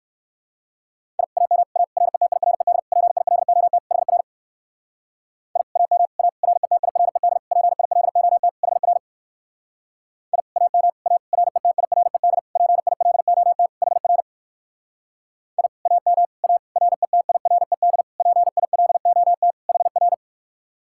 Morse code, heard as a keyed tone, about 56 wpm, 700 Hz. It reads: I AM A RETIRED PILOT HR I AM A RETIRED PILOT HR I AM A RETIRED PILOT HR I AM A RETIRED PILOT HR